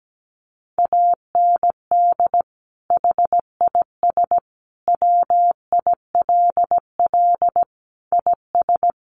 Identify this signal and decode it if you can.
Morse code, heard as a keyed tone, about 17 wpm, 700 Hz. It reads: AND HIS WILL IS
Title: Sentences from Top 1500 Words 17wpm